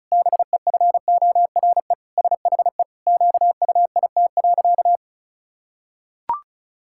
Morse code, heard as a keyed tone, about 34 wpm, 700 Hz. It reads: BEFORE SHE QUIT.
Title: Pangram Challenge v2 - 35wpm